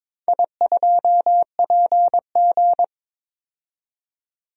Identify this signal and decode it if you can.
Morse code, heard as a keyed tone, about 22 wpm, 700 Hz. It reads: I2PG
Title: DXCC-100 Call Signs - 4 Characters - 22wpm